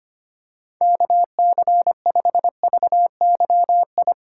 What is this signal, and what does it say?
Morse code, heard as a keyed tone, about 25 wpm, 700 Hz. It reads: KC5VYI
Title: USA Call Signs - 3x Character Spacing - 25wpm